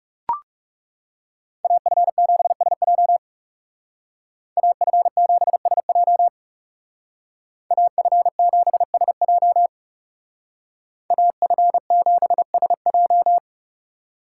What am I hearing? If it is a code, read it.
Morse code, heard as a keyed tone, about 37 wpm, 700 Hz. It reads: AF7SJ AF7SJ AF7SJ AF7SJ
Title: Speed Racing - USA 2x2 Call Signs 30wpm